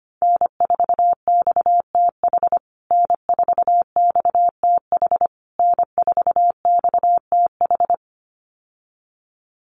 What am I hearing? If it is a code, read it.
Morse code, heard as a keyed tone, about 25 wpm, 700 Hz. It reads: N4XTH N4XTH N4XTH
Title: USA Call Signs Repeated 3x 25wpm